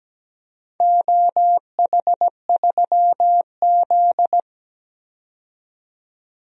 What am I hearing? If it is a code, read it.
Morse code, heard as a keyed tone, about 17 wpm, 700 Hz. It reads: OH3Z
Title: DXCC-100 Call Signs - 4 Characters - 17wpm